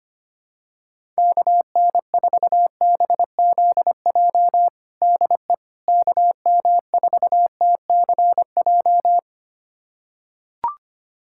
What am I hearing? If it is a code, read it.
Morse code, heard as a keyed tone, about 25 wpm, 700 Hz. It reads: KN4BZJ DE KM4TCJ
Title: USA Callsign DE Callsign - 6 Characters - 25wpm